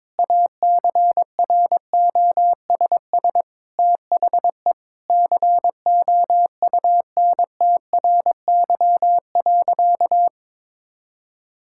Morse code, heard as a keyed tone, about 22 wpm, 700 Hz. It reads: ACROSS THE COUNTRY.
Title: Sentences from Top 1000 Words - Review Part 5 - 22wpm